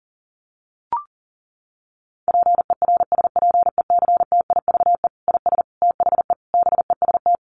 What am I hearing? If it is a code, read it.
Morse code, heard as a keyed tone, about 40 wpm, 700 Hz. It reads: PERSPECTIVE IS THE BEST
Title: Sentences from Top 1200 Words 40wpm